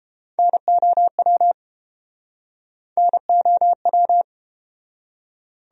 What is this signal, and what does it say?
Morse code, heard as a keyed tone, about 31 wpm, 700 Hz. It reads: NOW NOW